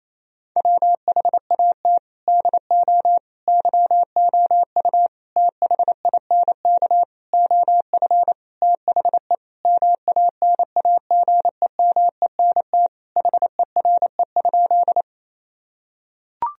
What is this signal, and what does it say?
Morse code, heard as a keyed tone, about 28 wpm, 700 Hz. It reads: WHAT DO YOU THINK OF THE MANAGEMENT HERE?
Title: Sentences from Top 1000 Words - Review Part 4 - 28wpm